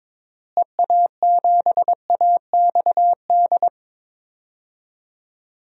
Morse code, heard as a keyed tone, about 22 wpm, 700 Hz. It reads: EA7AXD
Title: DXCC-100 Call Signs - 6 Characters - Repeated 3x - 22wpm